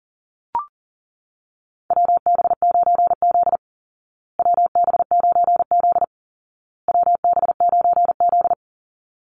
Morse code, heard as a keyed tone, about 40 wpm, 700 Hz. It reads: WB9Z WB9Z WB9Z
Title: Call Signs Repeated 2x - 40wpm